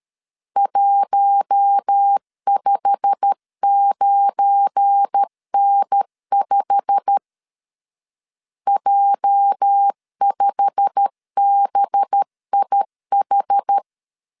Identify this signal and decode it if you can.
Morse code, heard as a keyed tone, about 13 wpm, 800 Hz. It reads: 159N5 J5BIH
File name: LEZIONI / 8 YJ45 / 8 YJ45 12wpm 2151